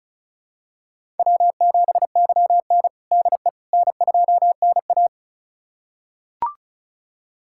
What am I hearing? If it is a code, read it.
Morse code, heard as a keyed tone, about 36 wpm, 700 Hz. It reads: W7YN DE N2NA
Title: USA Callsign DE Callsign - 4 Characters - 35wpm